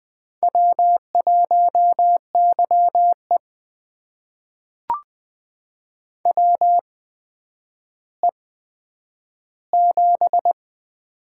Morse code, heard as a keyed tone, about 20 wpm, spaced as for 10 wpm, 700 Hz. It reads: W1YE WE7
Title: USA Call Signs - 4 Characters - 8x Character Spacing - 20wpm